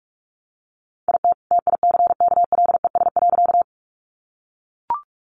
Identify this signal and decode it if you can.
Morse code, heard as a keyed tone, about 44 wpm, 700 Hz. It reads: IT TICKLES.
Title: Additional Easy Phrases 45wpm